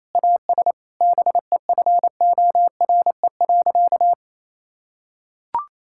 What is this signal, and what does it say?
Morse code, heard as a keyed tone, about 28 wpm, 700 Hz. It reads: AS BEFORE.
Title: Sentences from Top 1000 Words - Review Part 5 - 28wpm